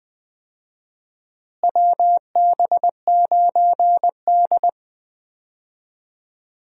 Morse code, heard as a keyed tone, about 20 wpm, 700 Hz. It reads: WB9D
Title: USA Call Signs - 4 Characters - Repeated 2x - 20wpm